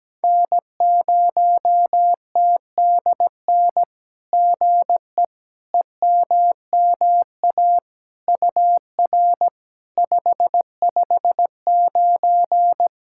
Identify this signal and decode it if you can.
Morse code, heard as a keyed tone, about 17 wpm, 700 Hz. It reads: N0TDN GE EMMA UR 559
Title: SOTA QSO Practice 17wpm